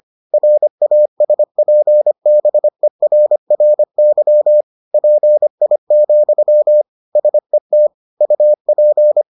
Morse code, heard as a keyed tone, about 25 wpm, 600 Hz. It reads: RASPBERRY PI, SET UP